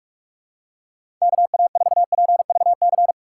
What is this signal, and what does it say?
Morse code, heard as a keyed tone, about 46 wpm, 700 Hz. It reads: KA4PVC
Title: USA Call Signs - 6 Characters - 2x Character Spacing - 45wpm